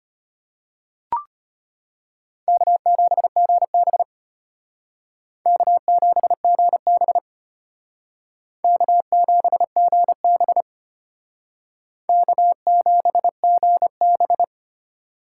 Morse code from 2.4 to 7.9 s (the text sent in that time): K7GB K7GB